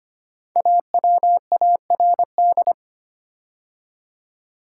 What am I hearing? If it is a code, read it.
Morse code, heard as a keyed tone, about 25 wpm, 700 Hz. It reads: AWARD
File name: 5 Letter Words - First 3 Letters Given - 25wpm